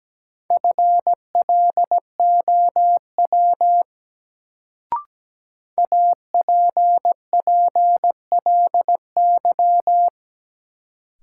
Morse code, heard as a keyed tone, about 17 wpm, 700 Hz. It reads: FLOW APPLY
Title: Sentences from Top 1200 Words - New Word Drill - 17wpm